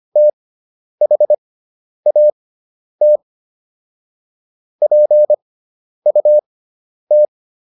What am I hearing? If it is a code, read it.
Morse code, heard as a keyed tone, about 25 wpm, spaced as for 10 wpm, 600 Hz. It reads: THAT PUT